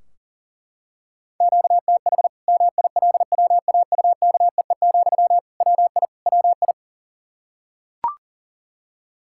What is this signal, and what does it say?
Morse code, heard as a keyed tone, about 40 wpm, 700 Hz. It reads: QTH MILWAUKEE, WI WI
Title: K1USN SST QSO Practice 40wpm